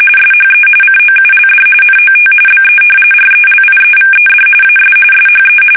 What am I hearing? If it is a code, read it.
RTTY, about 45.45 baud. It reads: SHE COULD NOT BE RELIEVED AS SHE
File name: fsk850 / battleshipnj